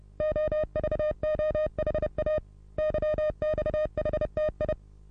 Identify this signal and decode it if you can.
Morse code, heard as a keyed tone, about 28 wpm, 610 Hz. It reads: OVOHA YXHTI
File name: repos / Morsekurs / DJ4UF Morsen27